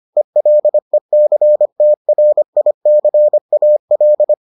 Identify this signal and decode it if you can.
Morse code, heard as a keyed tone, about 25 wpm, 600 Hz. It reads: ELECTRICAL